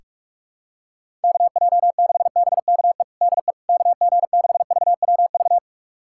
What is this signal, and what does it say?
Morse code, heard as a keyed tone, about 46 wpm, 700 Hz. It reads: KJ6BKE DE KG6VWV
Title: USA Callsign DE Callsign - 6 Characters - 45wpm